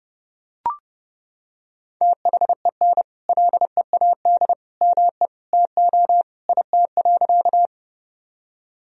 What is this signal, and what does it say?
Morse code, heard as a keyed tone, about 30 wpm, 700 Hz. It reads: THEN LEAD ME TO IT.
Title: Sentences from Top 1000 Words - Review Part 2 - 30wpm